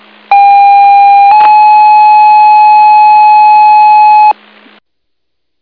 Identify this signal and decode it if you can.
Two-tone page, heard as a one-second tone, then a three-second tone, about 800 then 850 Hz. It is heard from 0.3 to 4.3 s.